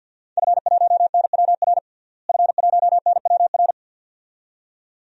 Morse code, heard as a keyed tone, about 46 wpm, 700 Hz. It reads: F1NWR F1NWR
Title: DXCC-100 Call Signs - 5 Characters - Repeated 2x - 50wpm